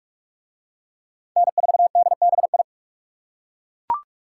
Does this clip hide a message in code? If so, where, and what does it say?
Yes: Morse code from 1.4 to 2.6 s, heard as a keyed tone, about 41 wpm, 700 Hz. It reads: N4DBI